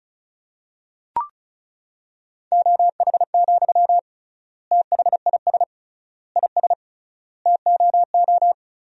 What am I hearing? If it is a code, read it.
Morse code, heard as a keyed tone, about 36 wpm, 700 Hz. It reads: OH, THIS IS TOO
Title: 3x Word-Spacing - Sentences from Top 500 Words - 35wpm